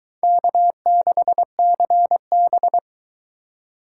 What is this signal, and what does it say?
Morse code, heard as a keyed tone, about 23 wpm, 700 Hz. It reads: K6CB